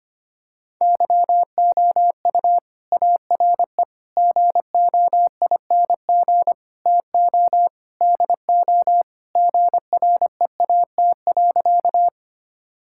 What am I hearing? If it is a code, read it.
Morse code, heard as a keyed tone, about 25 wpm, 700 Hz. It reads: YOU ARE GOING TO DO GREAT.
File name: practice / 25wpm / Sentences with 4-Word Phrases - 25wpm